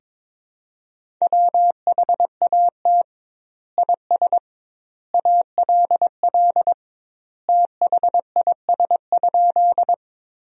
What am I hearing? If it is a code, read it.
Morse code, heard as a keyed tone, about 22 wpm, 700 Hz. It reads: WHAT IS ALL THIS?
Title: 2x Word-Spacing - Sentences from Top 100 Words - 22wpm